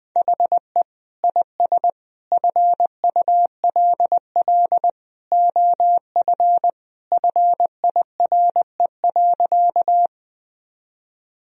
Morse code, heard as a keyed tone, about 21 wpm, 700 Hz. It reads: HE IS FULL OF FIRE.
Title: Sentences from Top 1000 Words - Review Part 3 - 20wpm